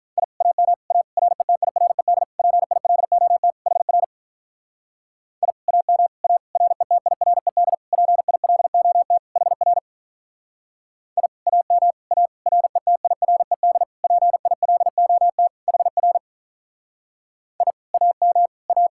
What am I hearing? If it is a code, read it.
Morse code, heard as a keyed tone, about 45 wpm, 700 Hz. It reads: I AM A RETIRED PILOT HR I AM A RETIRED PILOT HR I AM A RETIRED PILOT HR I AM A